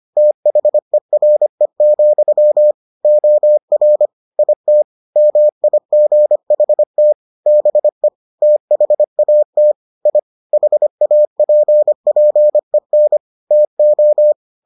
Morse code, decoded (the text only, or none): THERE, OR IT MIGHT BE THAT I HAPPEN TO